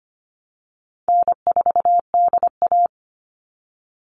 Morse code, heard as a keyed tone, about 25 wpm, 700 Hz. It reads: N4DA
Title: NA SOTA Chaser Callsigns 25wpm